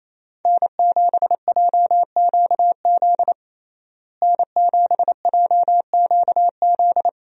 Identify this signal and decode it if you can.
Morse code, heard as a keyed tone, about 28 wpm, 700 Hz. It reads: N7JQZ N7JQZ
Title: USA Call Signs - 5 Characters - Repeated 3x - 28wpm